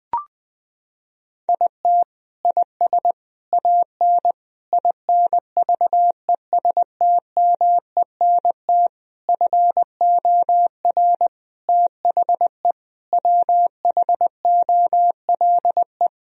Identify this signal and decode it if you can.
Morse code, heard as a keyed tone, about 20 wpm, 700 Hz. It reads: IT IS AN INVESTMENT FOR THE WHOLE
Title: Sentences from Top 1000 Words - Review Part 4 - 20wpm